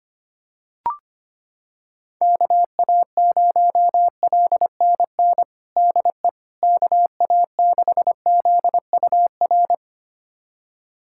Morse code, heard as a keyed tone, about 25 wpm, 700 Hz. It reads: KA0LNN DE KA6ZUR
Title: USA Callsign DE Callsign - 6 Characters - 25wpm